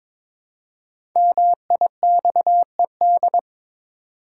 Morse code, heard as a keyed tone, about 22 wpm, 700 Hz. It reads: MIXED
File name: 5 Letter Words - First 3 Letters Given - 22wpm